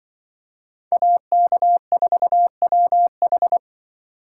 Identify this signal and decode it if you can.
Morse code, heard as a keyed tone, about 25 wpm, 700 Hz. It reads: AK4WH
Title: Speed Racing - USA 2x2 Call Signs 20wpm